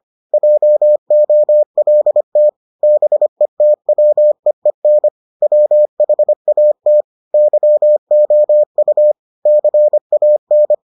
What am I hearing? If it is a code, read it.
Morse code, heard as a keyed tone, about 25 wpm, 600 Hz. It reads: JOLT BETWEEN WHAT YOU CAN